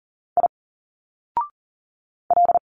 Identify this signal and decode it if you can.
Morse code, heard as a keyed tone, about 39 wpm, 700 Hz. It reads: I L
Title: Single Letters 40wpm